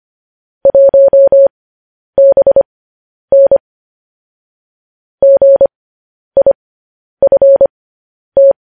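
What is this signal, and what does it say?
Morse code, heard as a keyed tone, about 25 wpm, spaced as for 10 wpm, 550 Hz. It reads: 1BN GIFT
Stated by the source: Morse code news headlines broadcast.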